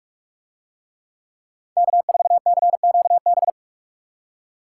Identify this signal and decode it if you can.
Morse code, heard as a keyed tone, about 45 wpm, 700 Hz. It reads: K4CQB